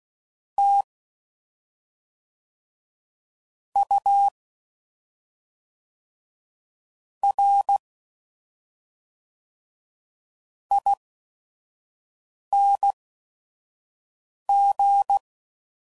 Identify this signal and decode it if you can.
Morse code, heard as a keyed tone, about 16 wpm, spaced as for 5 wpm, 800 Hz. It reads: T U R ING
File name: Suffixes / Special ING's / SlowToFaster ING's